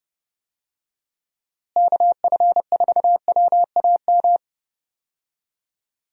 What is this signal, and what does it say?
Morse code, heard as a keyed tone, about 30 wpm, 700 Hz. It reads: KF4WAM
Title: USA Call Signs - 6 Characters - Repeated 2x - 30wpm